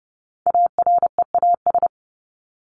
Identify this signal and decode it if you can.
Morse code, heard as a keyed tone, about 30 wpm, 700 Hz. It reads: AREAS
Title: ES:S Suffix - Top 100 Words with Base Words 30wpm